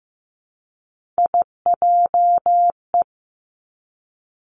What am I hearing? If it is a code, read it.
Morse code, heard as a keyed tone, about 15 wpm, 700 Hz. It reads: IJE
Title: Three Challenging Letter-Number Combinations 15wpm